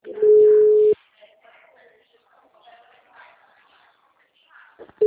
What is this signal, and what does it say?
Ringback tone (Europe, 425 Hz), with speech.
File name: out- -2003- - - .43785